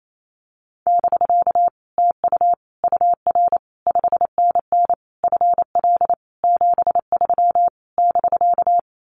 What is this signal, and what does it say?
Morse code, heard as a keyed tone, about 28 wpm, 700 Hz. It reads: <BK> TU UR 5NN FL 73 <BK>
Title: Common QSO Phrases Encore 28wpm